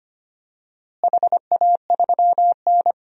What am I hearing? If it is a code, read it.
Morse code, heard as a keyed tone, about 25 wpm, 700 Hz. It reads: HA3N
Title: DXCC-100 Call Signs - 4 Characters - Repeated 3x - 25wpm